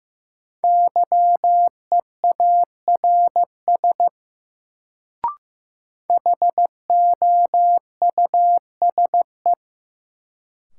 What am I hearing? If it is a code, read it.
Morse code, heard as a keyed tone, about 16 wpm, 700 Hz. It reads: YEARS HOUSE
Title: Sentences from Top 200 Words - New Word Drill - 15wpm